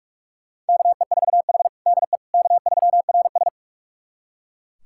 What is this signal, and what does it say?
Morse code, heard as a keyed tone, about 43 wpm, 700 Hz. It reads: KE4H DE K3RS
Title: USA Callsign DE Callsign - 4 Characters - 45wpm